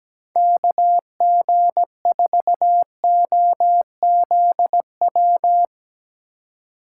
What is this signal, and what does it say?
Morse code, heard as a keyed tone, about 17 wpm, 700 Hz. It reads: KG4OZW
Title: USA Call Signs - 6 Characters - 3x Character Spacing - 17wpm